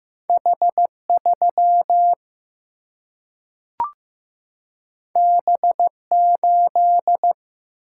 Morse code, heard as a keyed tone, about 15 wpm, 700 Hz. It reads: H3 B8
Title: Single-Letter Single-Number 15wpm